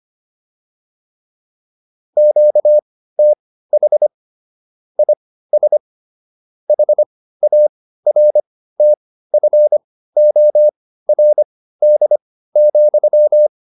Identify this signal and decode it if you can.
Morse code, heard as a keyed tone, about 25 wpm, spaced as for 15 wpm, 600 Hz. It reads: QTH IS HARTFORD,
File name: QSO401 15